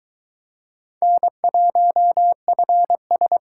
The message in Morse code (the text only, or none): N1FS